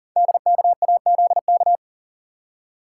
Morse code, heard as a keyed tone, about 40 wpm, 700 Hz. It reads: DKAZK
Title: Code Groups - 5 Letters - 40wpm